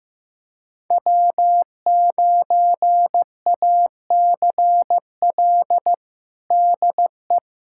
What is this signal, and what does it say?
Morse code, heard as a keyed tone, about 15 wpm, 700 Hz. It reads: W9ACL DE
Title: USA Callsign DE Callsign - 5 Characters - 15wpm